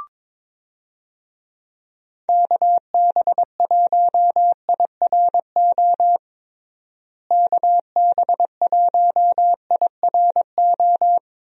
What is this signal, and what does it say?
Morse code, heard as a keyed tone, about 22 wpm, 700 Hz. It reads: KB1IRO KB1IRO
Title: USA Call Signs - 6 Characters - Repeated 3x - 22wpm